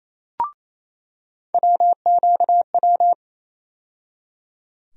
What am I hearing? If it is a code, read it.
Morse code, heard as a keyed tone, about 28 wpm, 700 Hz. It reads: WQW
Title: USA Call Sign Postfixes - 3 Letters 28wpm